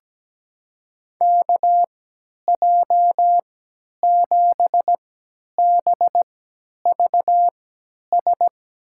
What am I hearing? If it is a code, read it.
Morse code, heard as a keyed tone, about 17 wpm, spaced as for 10 wpm, 700 Hz. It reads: KJ7BVS
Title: USA Call Signs - 3x Character Spacing - 17wpm